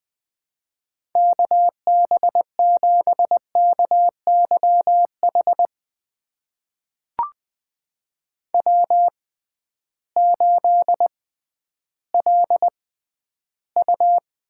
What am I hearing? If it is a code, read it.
Morse code, heard as a keyed tone, about 20 wpm, spaced as for 12 wpm, 700 Hz. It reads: KB7KYH W8LU
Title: USA Call Signs - 6x Character Spacing - 20wpm